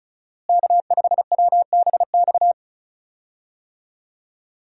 Morse code, heard as a keyed tone, about 36 wpm, 700 Hz. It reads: K5WBX
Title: USA Call Signs - 5 Characters - Repeated 2x - 35wpm